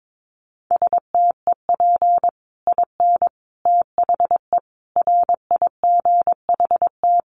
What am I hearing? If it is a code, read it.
Morse code, heard as a keyed tone, about 22 wpm, 700 Hz. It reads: STEP IN THE RIGHT
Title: Sets of 5 Words Encore 22wpm